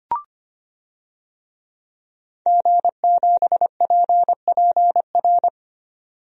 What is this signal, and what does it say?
Morse code, heard as a keyed tone, about 25 wpm, 700 Hz. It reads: G7PPR
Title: DXCC-100 Call Signs - 5 Characters - 25wpm